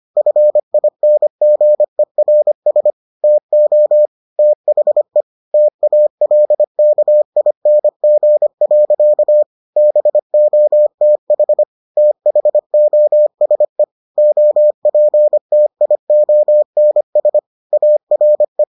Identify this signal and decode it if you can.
Morse code, heard as a keyed tone, about 25 wpm, 600 Hz. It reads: FINGERS TO THE TALKING. BOTH THOSE OPTIONS ARE